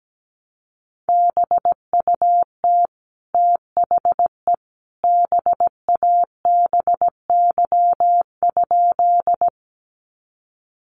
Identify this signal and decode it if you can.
Morse code, heard as a keyed tone, about 17 wpm, 700 Hz. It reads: BUT THE BABY?
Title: Sentences from Top 1300 Words 17wpm